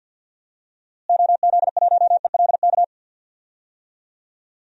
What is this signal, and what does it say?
Morse code, heard as a keyed tone, about 49 wpm, 700 Hz. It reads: OZ1ELK
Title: DXCC-100 Call Signs - 6 Characters - Repeated 2x - 50wpm